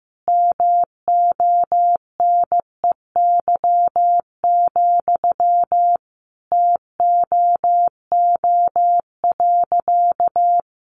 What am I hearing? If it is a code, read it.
Morse code, heard as a keyed tone, about 15 wpm, 700 Hz. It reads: MONEY, TOO.
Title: Sentences from Top 900 Words 15wpm